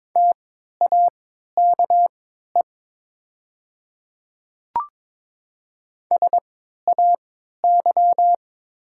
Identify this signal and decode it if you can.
Morse code, heard as a keyed tone, about 22 wpm, spaced as for 12 wpm, 700 Hz. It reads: TAKE SAY
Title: 3x Character Spacing - Top 500 Words - 22wpm